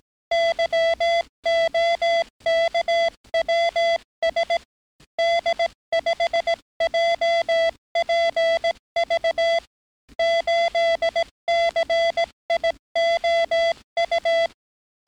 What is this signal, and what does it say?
Morse code, heard as a keyed tone, about 18 wpm, 670 Hz. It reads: YOKWS D5JPV 8CIOU
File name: training / NAVPERS 11049to11060 / NAVPERS 11058 RA RB